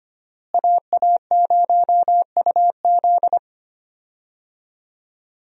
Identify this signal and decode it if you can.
Morse code, heard as a keyed tone, about 25 wpm, 700 Hz. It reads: AA0UZ
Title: USA Call Signs - 5 Characters - Repeated 2x - 25wpm